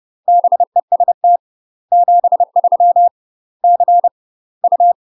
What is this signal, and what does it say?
Morse code, heard as a keyed tone, about 30 wpm, 700 Hz. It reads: BEST 73 C U
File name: UK MASTER QSOs TRAINING / 30WPM / UK MASTER QSOs 0034